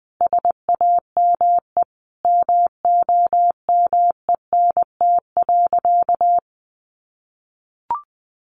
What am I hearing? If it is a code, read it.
Morse code, heard as a keyed tone, about 20 wpm, 700 Hz. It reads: SAME MOMENT.
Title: Sentences from Top 1000 Words 20wpm